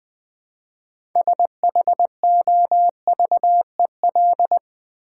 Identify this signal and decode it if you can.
Morse code, heard as a keyed tone, about 20 wpm, 700 Hz. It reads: SHOVEL
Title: Easy Words with Hints - 20wpm